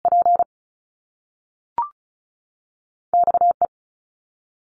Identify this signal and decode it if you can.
Morse code, heard as a keyed tone, about 36 wpm, spaced as for 6 wpm, 700 Hz. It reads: PXE
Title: Top Two Letter Combinations in Words 35wpm